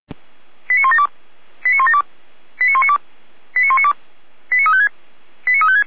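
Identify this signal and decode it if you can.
Five-tone selective call (ZVEI), heard as quick runs of five tones, tones of about 70 ms. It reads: 89172 89172 89192 89192 89356 89356
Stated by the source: ZVEI five-tone call received with an SDR.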